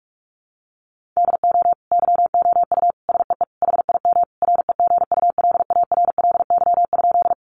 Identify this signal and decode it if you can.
Morse code, heard as a keyed tone, about 44 wpm, 700 Hz. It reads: DO YOU SEE HIM REGULARLY?
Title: Sentences from Top 1400 Words 45wpm